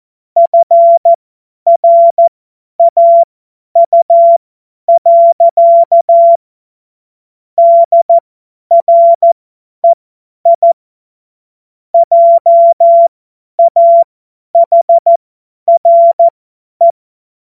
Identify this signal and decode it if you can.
Morse code, heard as a keyed tone, about 14 wpm, spaced as for 10 wpm, 680 Hz. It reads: FRAU. DREI JAHRE
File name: Morsekurs / Witze / Witz 05 März 2024 50 F70